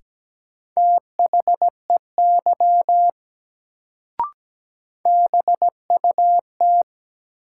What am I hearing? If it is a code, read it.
Morse code, heard as a keyed tone, about 17 wpm, 700 Hz. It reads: THEY BUT
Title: Sentences from Top 100 Words - New Word Drill - 17wpm